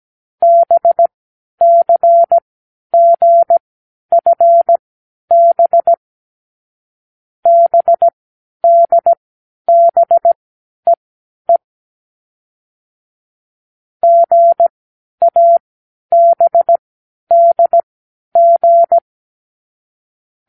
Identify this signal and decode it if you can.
Morse code, heard as a keyed tone, about 17 wpm, spaced as for 10 wpm, 680 Hz. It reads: BCGFB BDBEE GABDG
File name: k7qo / Lesson12(A-H)